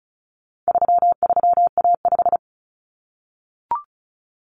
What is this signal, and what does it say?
Morse code, heard as a keyed tone, about 34 wpm, 700 Hz. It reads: 33A5
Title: Common Cut Numbers - 4-Digit 35wpm